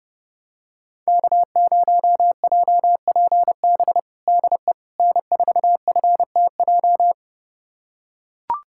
Morse code, heard as a keyed tone, about 30 wpm, 700 Hz. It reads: K0JPB DE N4FTJ
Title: USA Callsign DE Callsign - 5 Characters - 30wpm